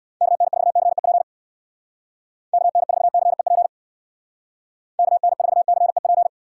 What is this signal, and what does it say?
Morse code, heard as a keyed tone, about 65 wpm, 700 Hz. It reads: KN4CEP KN4CEP KN4CEP